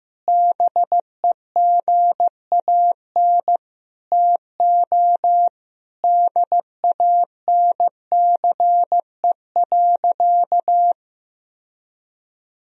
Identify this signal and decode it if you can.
Morse code, heard as a keyed tone, about 15 wpm, 700 Hz. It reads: BEGAN TO DANCE.
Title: Sentences from Top 1000 Words - Review Part 4 - 15wpm